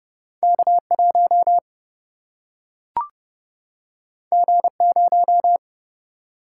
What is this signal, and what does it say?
Morse code, heard as a keyed tone, about 30 wpm, 700 Hz. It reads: K 1 G 0
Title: Single-Letter Single-Number 30wpm